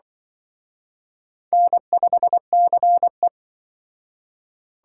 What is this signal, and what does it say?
Morse code, heard as a keyed tone, about 24 wpm, 700 Hz. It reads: N5CE
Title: Speed Racing - USA 1x2 Call Signs 20wpm